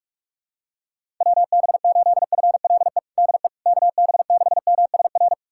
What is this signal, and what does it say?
Morse code, heard as a keyed tone, about 44 wpm, 700 Hz. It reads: WB8FLE DE KB6MSR
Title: USA Callsign DE Callsign - 6 Characters - 45wpm